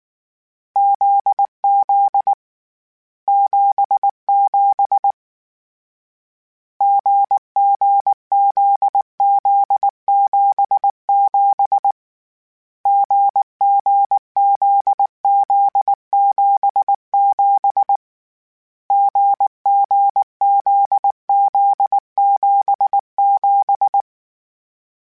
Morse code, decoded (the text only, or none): ZZ 77 GGZZ77 GGZZ77 GGZZ77